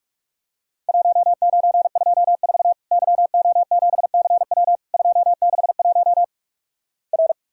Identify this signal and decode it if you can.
Morse code, heard as a keyed tone, about 46 wpm, 700 Hz. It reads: 1924 YO7CW 261 R
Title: Worked All Europe DX Contest - QTC Exchange - 45wpm